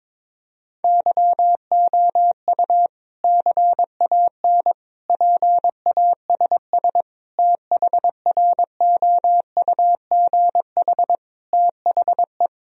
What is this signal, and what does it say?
Morse code, heard as a keyed tone, about 22 wpm, 700 Hz. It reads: YOU CAN PASS THROUGH THE
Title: Sentences from Top 1000 Words - Review Part 1 - 22wpm